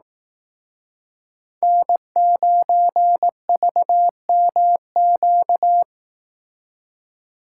Morse code, heard as a keyed tone, about 18 wpm, 700 Hz. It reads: N9VMQ
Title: Speed Racing - USA 1x3 Call Signs 15wpm